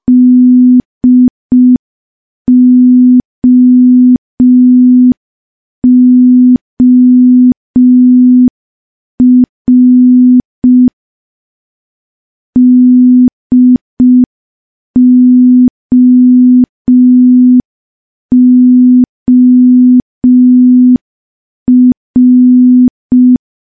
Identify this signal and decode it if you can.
Morse code, heard as a keyed tone, about 5 wpm, 250 Hz. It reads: DOOR DOOR
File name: morse / morse-training-05WPM